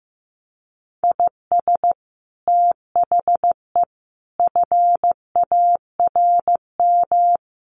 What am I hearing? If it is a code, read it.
Morse code, heard as a keyed tone, about 15 wpm, 700 Hz. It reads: IS THE FARM
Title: Sentences from Top 1400 Words 15wpm